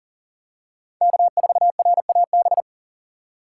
Morse code, heard as a keyed tone, about 37 wpm, 700 Hz. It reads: K4RAB
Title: K1USN SST QSO Practice - 40wpm